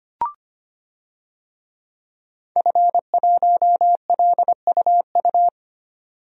Morse code, heard as a keyed tone, about 25 wpm, 700 Hz. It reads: F1LUU
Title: DXCC-100 Call Signs - 5 Characters - 25wpm